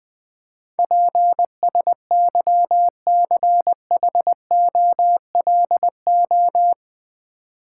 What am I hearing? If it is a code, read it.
Morse code, heard as a keyed tone, about 20 wpm, 700 Hz. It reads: PSYCHOLO
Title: Build A Word - 4 Letters At A Time - 12-Letter Words - 20wpm